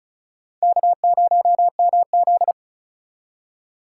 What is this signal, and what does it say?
Morse code, heard as a keyed tone, about 49 wpm, spaced as for 42 wpm, 700 Hz. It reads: K0MZ
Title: Speed Racing - USA Call Signs 35wpm